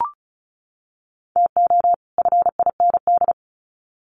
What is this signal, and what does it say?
Morse code, heard as a keyed tone, about 34 wpm, 700 Hz. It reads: TO FIND
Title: Sets of 2 Words 35wpm